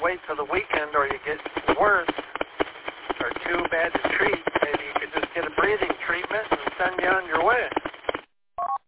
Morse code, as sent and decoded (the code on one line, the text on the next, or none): none